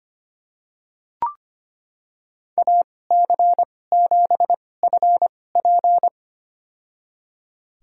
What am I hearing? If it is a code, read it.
Morse code, heard as a keyed tone, about 25 wpm, spaced as for 18 wpm, 700 Hz. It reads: AC7FP
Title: USA Call Signs - 5 Characters - 2x Character Spacing - 25wpm